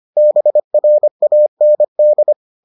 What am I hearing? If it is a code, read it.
Morse code, heard as a keyed tone, about 25 wpm, 600 Hz. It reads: BRAND